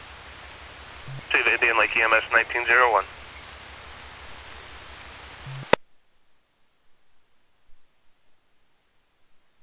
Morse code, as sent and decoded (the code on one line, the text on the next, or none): none